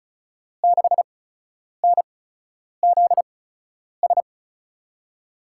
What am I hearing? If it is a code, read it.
Morse code, heard as a keyed tone, about 36 wpm, spaced as for 10 wpm, 700 Hz. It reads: 6NZS
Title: USA Call Signs - 5 Characters - 8x Character Spacing - 35wpm